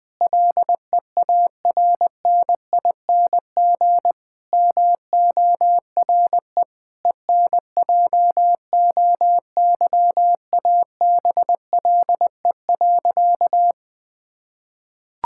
Morse code, heard as a keyed tone, about 20 wpm, 700 Hz. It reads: LEARNING MORE ENJOYABLE.
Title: Difficult Sentences with Hints - 20wpm